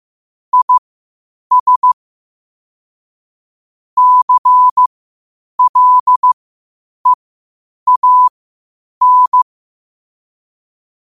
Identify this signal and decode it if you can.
Morse code, heard as a keyed tone, about 15 wpm, spaced as for 8 wpm, 1000 Hz. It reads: IS CLEAN
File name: qotdfiles / 8wpm